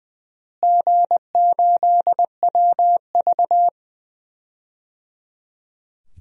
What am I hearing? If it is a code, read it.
Morse code, heard as a keyed tone, about 21 wpm, 700 Hz. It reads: G8WV
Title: DXCC-100 Call Signs - 4 Characters - Repeated 3x - 20wpm